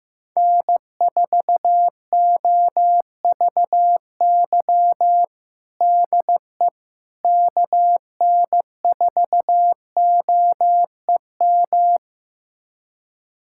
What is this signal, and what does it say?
Morse code, heard as a keyed tone, about 15 wpm, 700 Hz. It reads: N4OVY DE KN4OEM
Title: USA Callsign DE Callsign 15wpm